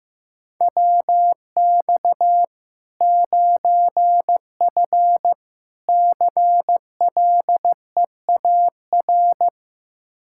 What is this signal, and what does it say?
Morse code, heard as a keyed tone, about 15 wpm, 700 Hz. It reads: WX 9F CLEAR
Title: K1USN SST QSO Practice 15wpm